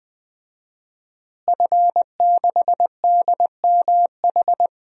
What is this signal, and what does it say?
Morse code, heard as a keyed tone, about 20 wpm, 700 Hz. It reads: F6DMH
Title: DXCC-100 Call Signs - 5 Characters - Repeated 3x - 20wpm